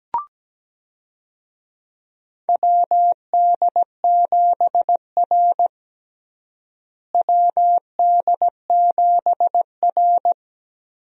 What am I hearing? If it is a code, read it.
Morse code, heard as a keyed tone, about 17 wpm, 700 Hz. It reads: WD7R WD7R
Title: USA Call Signs - 4 Characters - Repeated 3x - 17wpm